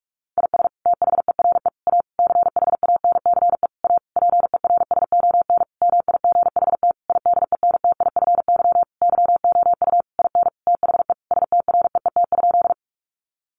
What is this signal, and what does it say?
Morse code, heard as a keyed tone, about 44 wpm, 700 Hz. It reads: IS THERE A CHANCE A PERSON MIGHT IDENTIFY YOU IN THE STREET?
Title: Sentences from Top 900 Words 45wpm